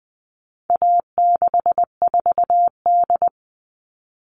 Morse code, heard as a keyed tone, about 20 wpm, 700 Hz. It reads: A64D
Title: All Cut Numbers - 4-Digit 20wpm